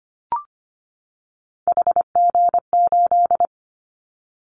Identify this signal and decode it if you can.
Morse code, heard as a keyed tone, about 25 wpm, 700 Hz. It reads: HG8
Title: Callsign Prefixes for Top 50 DXCC - 25wpm